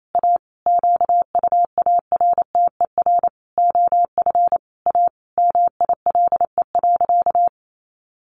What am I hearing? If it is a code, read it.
Morse code, heard as a keyed tone, about 28 wpm, 700 Hz. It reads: A QUARTER OF A MILE.
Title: Sentences from Top 1300 Words 28wpm